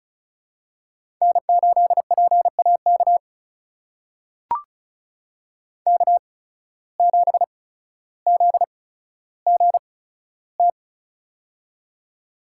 Morse code, heard as a keyed tone, about 36 wpm, spaced as for 15 wpm, 700 Hz. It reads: N8PAK K7ZGT
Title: USA Call Signs - 5 Characters - 8x Character Spacing - 35wpm